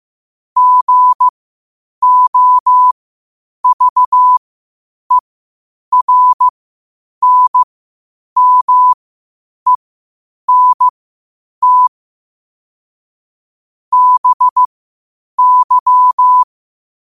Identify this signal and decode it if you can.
Morse code, heard as a keyed tone, about 15 wpm, spaced as for 8 wpm, 1000 Hz. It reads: GOVERNMENT BY
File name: qotdfiles / 8wpm